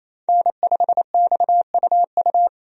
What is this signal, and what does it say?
Morse code, heard as a keyed tone, about 27 wpm, 700 Hz. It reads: N5XUU
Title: USA Call Signs - 5 Characters - Repeated 2x - 28wpm